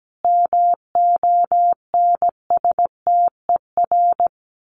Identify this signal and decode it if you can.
Morse code, heard as a keyed tone, about 17 wpm, 700 Hz. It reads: MONSTER
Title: A Sound Of Thunder 17wpm